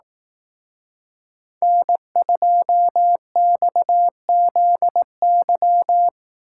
Morse code, heard as a keyed tone, about 18 wpm, 700 Hz. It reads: N2XZY
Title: Speed Racing - USA 1x3 Call Signs 15wpm